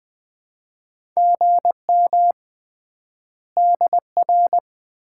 Morse code, heard as a keyed tone, about 20 wpm, 700 Hz. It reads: GM DR
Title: Common QSO Phrases - 3x Word-Spacing - 20wpm